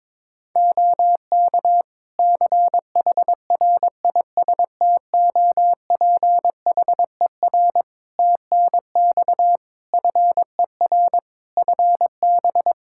Morse code, heard as a keyed tone, about 21 wpm, 700 Hz. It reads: OK CHRISTOPHER TNX FER FB
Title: Practice QSO - Exchange 4 - Ending - 22wpm